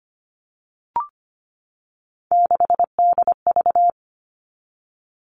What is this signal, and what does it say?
Morse code, heard as a keyed tone, about 25 wpm, 700 Hz. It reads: 6DV
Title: All Cut Numbers - 3-Digit 25wpm